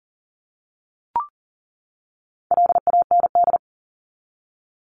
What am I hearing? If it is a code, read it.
Morse code, heard as a keyed tone, about 40 wpm, 700 Hz. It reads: LAND
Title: Top 300 Words 40wpm